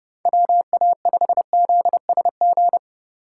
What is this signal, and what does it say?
Morse code, heard as a keyed tone, about 30 wpm, 700 Hz. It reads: WA5ZSG
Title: USA Call Signs - 3x Character Spacing - 30wpm